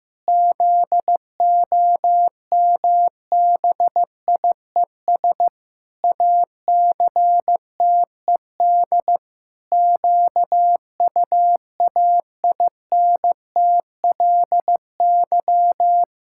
Morse code, heard as a keyed tone, about 15 wpm, 700 Hz. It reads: ZOMBIES ACTED QUAINTLY
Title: Pangram Challenge v2 - 15wpm